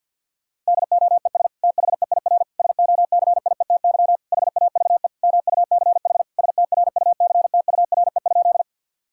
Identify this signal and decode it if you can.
Morse code, heard as a keyed tone, about 50 wpm, 700 Hz. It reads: DOES THEIR SOCIETY HAVE MUCH STRUCTURE?
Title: Sentences from Top 1000 Words - Review Part 6 - 50wpm